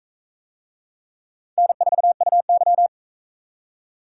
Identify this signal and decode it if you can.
Morse code, heard as a keyed tone, about 41 wpm, 700 Hz. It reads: N4UY